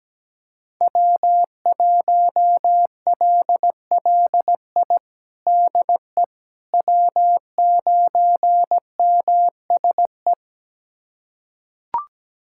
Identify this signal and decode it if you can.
Morse code, heard as a keyed tone, about 17 wpm, 700 Hz. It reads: W1LLI DE W9MSE
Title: USA Callsign DE Callsign - 5 Characters - 17wpm